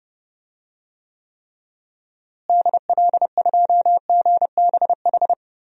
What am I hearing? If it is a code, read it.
Morse code, heard as a keyed tone, about 30 wpm, 700 Hz. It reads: DL2GBH
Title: DXCC-100 Call Signs - 6 Characters - Repeated 3x - 30wpm